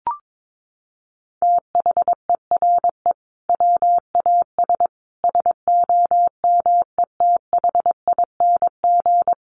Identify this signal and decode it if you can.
Morse code, heard as a keyed tone, about 22 wpm, 700 Hz. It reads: THERE WAS SOMETHING
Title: Sentences from Top 1500 Words 22wpm